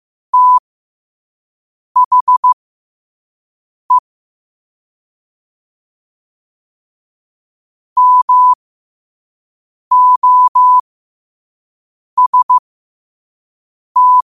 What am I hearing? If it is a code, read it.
Morse code, heard as a keyed tone, about 15 wpm, spaced as for 5 wpm, 1000 Hz. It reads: THE MOST
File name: qotdfiles / 5wpm